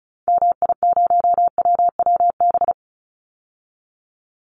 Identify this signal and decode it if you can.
Morse code, heard as a keyed tone, about 34 wpm, 700 Hz. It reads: MI0WWB
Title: Call Signs Repeated 3x - 35wpm